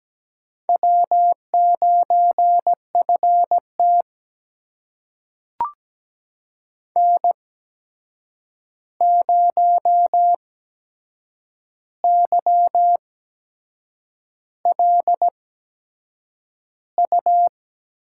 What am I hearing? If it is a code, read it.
Morse code, heard as a keyed tone, about 17 wpm, spaced as for 7 wpm, 700 Hz. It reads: W9FT N0YLU
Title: USA Call Signs - 8x Character Spacing - 17wpm